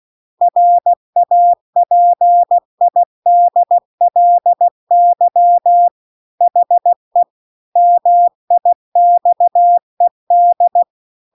Morse code, heard as a keyed tone, about 16 wpm, 700 Hz. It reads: RAPIDLY HE MIXED